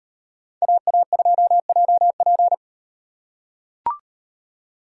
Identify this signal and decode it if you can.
Morse code, heard as a keyed tone, about 38 wpm, 700 Hz. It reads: AA2JP